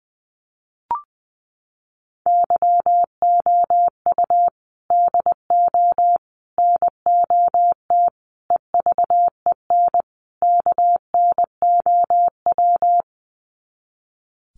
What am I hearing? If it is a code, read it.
Morse code, heard as a keyed tone, about 20 wpm, 700 Hz. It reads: YOU DO NOT EVEN KNOW
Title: Sets of 5 Words Encore2 20wpm